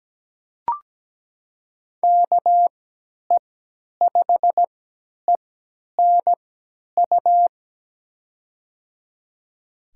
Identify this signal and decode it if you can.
Morse code, heard as a keyed tone, about 17 wpm, spaced as for 10 wpm, 700 Hz. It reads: KE5ENU
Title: USA Call Signs - 6 Characters - 3x Character Spacing - 17wpm